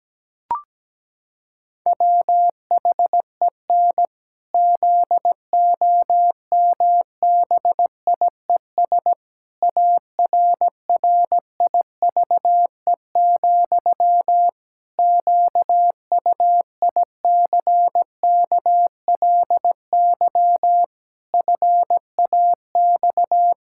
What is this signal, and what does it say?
Morse code, heard as a keyed tone, about 17 wpm, 700 Hz. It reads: WHEN ZOMBIES ARRIVE, QUICKLY FAX
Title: Pangram Challenge v2 - 17wpm